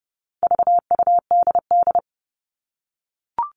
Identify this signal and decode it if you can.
Morse code, heard as a keyed tone, about 28 wpm, 700 Hz. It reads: VUDD
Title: All Cut Numbers - 4-Digit 30wpm